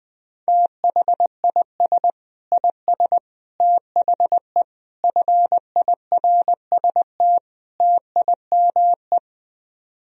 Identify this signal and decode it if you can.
Morse code, heard as a keyed tone, about 22 wpm, 700 Hz. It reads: THIS IS THE FIRST TIME
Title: Speed-Racing - Sets of 5 Words 20wpm